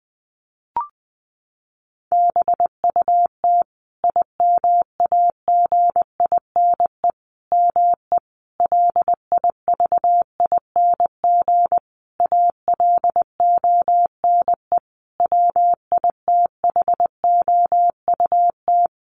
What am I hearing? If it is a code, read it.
Morse code, heard as a keyed tone, about 20 wpm, 700 Hz. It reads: BUT IMAGINE ME LIVING ALONE WITHOUT
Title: Sentences from Top 1400 Words 20wpm